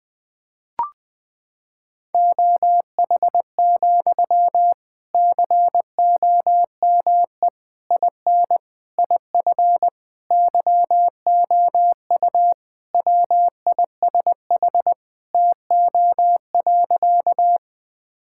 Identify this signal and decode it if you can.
Morse code, heard as a keyed tone, about 21 wpm, 700 Hz. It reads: OH, COME IN IF YOU WISH TO.
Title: Sentences from Top 1600 Words 20wpm